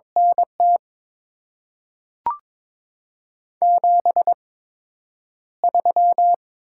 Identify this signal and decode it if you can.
Morse code, heard as a keyed tone, about 21 wpm, spaced as for 9 wpm, 700 Hz. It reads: NT 73
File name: Top 20 QSO Elements - 8x Character Spacing - 22wpm